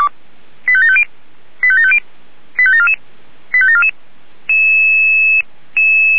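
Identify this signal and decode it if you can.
Five-tone selective call (ZVEI), heard as quick runs of five tones, tones of about 70 ms. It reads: 85750 85750 85744 85744
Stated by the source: ZVEI five-tone call received on an SDR.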